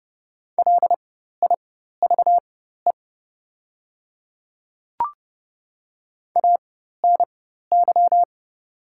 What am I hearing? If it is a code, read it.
Morse code, heard as a keyed tone, about 30 wpm, spaced as for 14 wpm, 700 Hz. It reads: LIVE ANY
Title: 4x Character Spacing - Top 200 Words - 30wpm